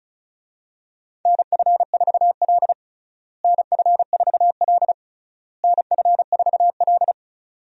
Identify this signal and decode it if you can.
Morse code, heard as a keyed tone, about 34 wpm, 700 Hz. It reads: NF4L NF4L NF4L
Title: USA Call Signs - 4 Characters - Repeated 3x - 35wpm